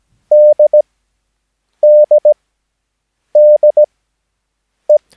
Morse code, heard as a keyed tone, about 17 wpm, spaced as for 7 wpm, 600 Hz. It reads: DDDE
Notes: Övningsmaterial med alfabetet A till Ö samt siffror 0 till 9 enligt International/Continental Morse Code.